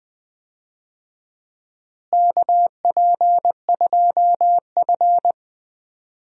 Morse code, heard as a keyed tone, about 20 wpm, 700 Hz. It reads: KP2F
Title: DXCC-100 Call Signs - 4 Characters - Repeated 2x - 20wpm